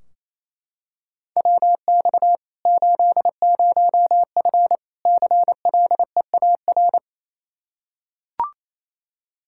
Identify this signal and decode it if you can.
Morse code, heard as a keyed tone, about 28 wpm, 700 Hz. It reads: WX 80F CLEAR
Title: Practice QSO - Exchange 2 - Info - 28wpm